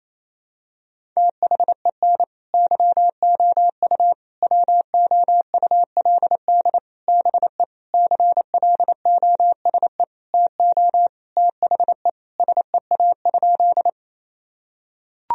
Morse code, heard as a keyed tone, about 28 wpm, 700 Hz. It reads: THEN YOU WOULD BE CLOSE TO THE SEA?
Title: Sentences from Top 1000 Words - Review Part 1 - 28wpm